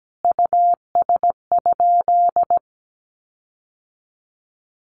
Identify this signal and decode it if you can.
Morse code, heard as a keyed tone, about 17 wpm, 700 Hz. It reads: US?
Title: Sentences from Top 1500 Words 17wpm